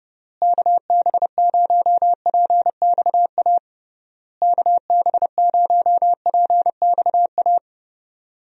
Morse code, heard as a keyed tone, about 32 wpm, 700 Hz. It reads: KB0PXA KB0PXA
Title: USA Call Signs - 6 Characters - Repeated 3x - 30wpm